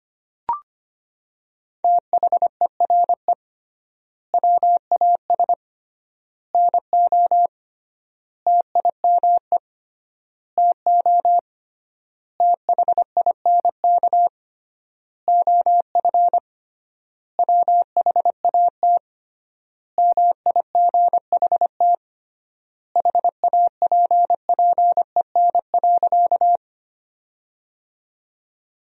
Morse code, heard as a keyed tone, about 25 wpm, 700 Hz. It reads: THERE WAS NO TIME TO THINK OF WHAT MIGHT HAPPEN.
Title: 3x Word-Spacing - Sentences from Top 400 Words - 25wpm